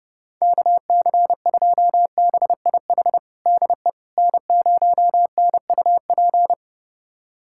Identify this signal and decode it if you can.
Morse code, heard as a keyed tone, about 32 wpm, 700 Hz. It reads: KC2BIH DE N0NUP
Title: USA Callsign DE Callsign 30wpm